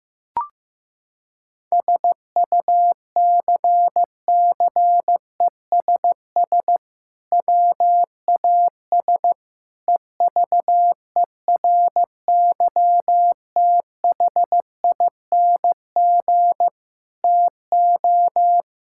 Morse code, heard as a keyed tone, about 15 wpm, 700 Hz. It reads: SUCCESS WAS EVERYTHING TO
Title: Sentences from Top 1000 Words - Review Part 6 - 15wpm